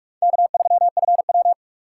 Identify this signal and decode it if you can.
Morse code, heard as a keyed tone, about 47 wpm, 700 Hz. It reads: K3FW
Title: USA Call Signs - 4 Characters - 4x Character Spacing - 45wpm